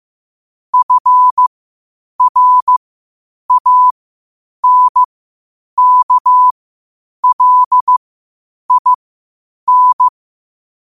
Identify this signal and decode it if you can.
Morse code, heard as a keyed tone, about 15 wpm, spaced as for 8 wpm, 1000 Hz. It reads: FRANKLIN
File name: qotdfiles / 8wpm